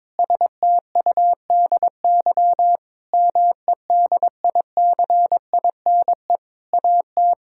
Morse code, heard as a keyed tone, about 22 wpm, 700 Hz. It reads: STUDY MEDICINE AT
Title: Sentences from Top 1000 Words - Review Part 6 - 22wpm